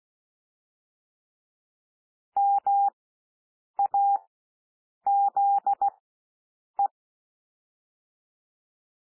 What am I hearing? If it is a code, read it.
Morse code, heard as a keyed tone, about 16 wpm, spaced as for 7 wpm, 800 Hz. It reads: MAZE
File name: SOUND WORDS 16 wpm